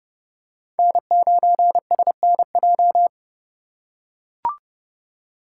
Morse code, heard as a keyed tone, about 32 wpm, 690 Hz. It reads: N9SNJ
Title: USA Call Signs - 4x Character Spacing - 30wpm